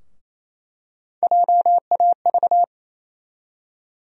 Morse code, heard as a keyed tone, about 28 wpm, 700 Hz. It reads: JAV
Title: USA Call Sign Postfixes - 3 Letters 28wpm